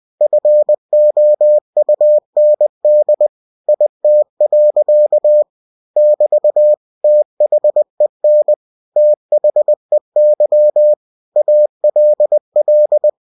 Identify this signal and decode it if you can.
Morse code, heard as a keyed tone, about 20 wpm, 600 Hz. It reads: FOUND IT. <BT> THEN THEY ALL